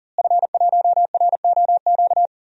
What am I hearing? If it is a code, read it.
Morse code, heard as a keyed tone, about 40 wpm, 700 Hz. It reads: F1ROQ
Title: DXCC-100 Call Signs - 5 Characters - Repeated 3x - 40wpm